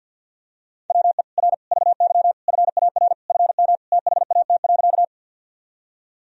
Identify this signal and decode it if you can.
Morse code, heard as a keyed tone, about 52 wpm, 700 Hz. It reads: WE R VY FAR FM THAT.
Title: Sentences from Top 300 Words and Abbreviations 50wpm